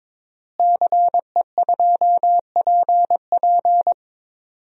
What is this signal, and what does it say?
Morse code, heard as a keyed tone, about 22 wpm, 700 Hz. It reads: CE2PP
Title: DXCC-100 Call Signs - 5 Characters - Repeated 3x - 22wpm